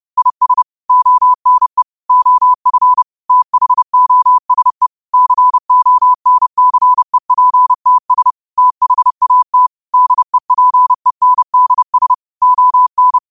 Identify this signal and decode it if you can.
Morse code, heard as a keyed tone, about 30 wpm, 1000 Hz. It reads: IS ONE OF THOSE CONCEPTS THAT DEPENDS ON
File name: qotdfiles / 30wpm